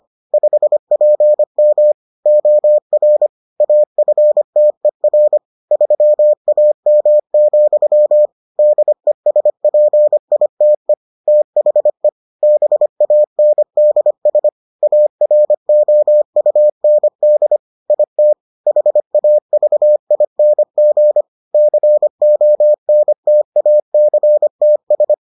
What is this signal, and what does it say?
Morse code, heard as a keyed tone, about 25 wpm, 600 Hz. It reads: 5PM OR AFTER 3AM, DESPITE THE BANDS AROUND IT HAVING CONTACTS